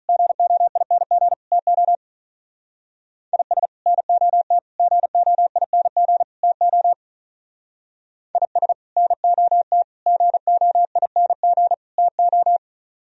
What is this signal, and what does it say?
Morse code, heard as a keyed tone, about 42 wpm, 700 Hz. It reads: GOING TO IS NOT GOING TO IS NOT GOING TO
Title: Speed-Racing - Sets of 4 Words 35wpm